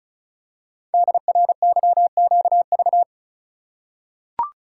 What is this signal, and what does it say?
Morse code, heard as a keyed tone, about 35 wpm, 700 Hz. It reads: DRYQV
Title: Code Groups - 5 Letters - 35wpm